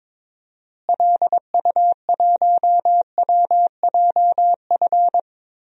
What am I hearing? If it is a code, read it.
Morse code, heard as a keyed tone, about 22 wpm, 700 Hz. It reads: LU1WJF
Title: DXCC-100 Call Signs - 6 Characters - Repeated 2x - 22wpm